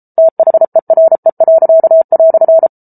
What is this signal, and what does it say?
Morse code, heard as a keyed tone, about 34 wpm, 650 Hz. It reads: THERE."